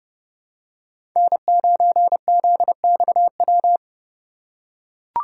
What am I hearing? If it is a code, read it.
Morse code, heard as a keyed tone, about 30 wpm, 700 Hz. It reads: N9ZXW
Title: USA Call Signs - 3x Character Spacing - 30wpm